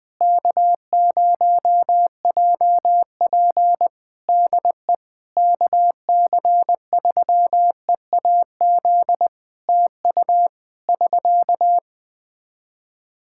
Morse code, heard as a keyed tone, about 20 wpm, 700 Hz. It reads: K0JP DE KC3EAZ TU <SK>
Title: Practice QSO - Exchange 4 - Ending - 20wpm